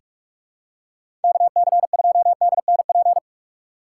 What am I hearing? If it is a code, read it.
Morse code, heard as a keyed tone, about 46 wpm, 700 Hz. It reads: K C 2 D N P
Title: USA Call Signs - 6 Characters - 6x Character Spacing - 45wpm